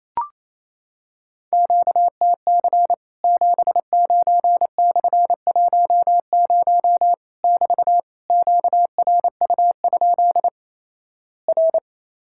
Morse code, heard as a keyed tone, about 28 wpm, 700 Hz. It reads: QTC 79/10 <BT> QRU? R
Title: Worked All Europe DX Contest - QTC Exchange - 28wpm